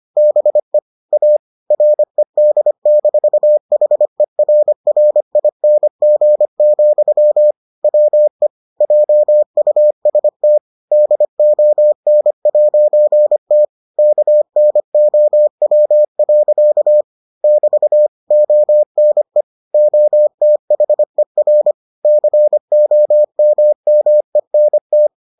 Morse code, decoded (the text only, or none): BE A RED-HERRING, WE JUST DON'T KNOW. <BT> ONE OTHER COMMENT